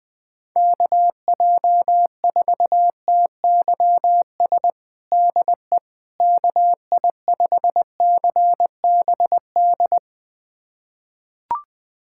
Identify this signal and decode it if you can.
Morse code, heard as a keyed tone, about 21 wpm, 700 Hz. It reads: KJ4TYS DE KI5CBD
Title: USA Callsign DE Callsign - 6 Characters - 20wpm